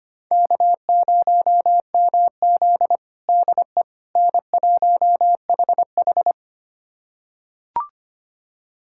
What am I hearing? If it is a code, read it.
Morse code, heard as a keyed tone, about 25 wpm, 700 Hz. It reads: K0MZ DE N1HH
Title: USA Callsign DE Callsign - 4 Characters - 25wpm